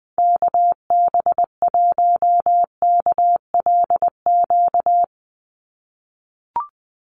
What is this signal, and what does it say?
Morse code, heard as a keyed tone, about 20 wpm, 700 Hz. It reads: KB1KLQ
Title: USA Call Signs 20wpm